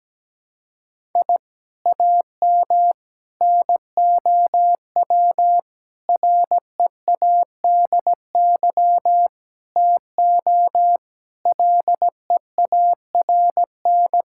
Morse code, heard as a keyed tone, about 17 wpm, 700 Hz. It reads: I AM NOW READY TO LEARN
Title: Sentences from Top 1000 Words - Review Part 6 - 17wpm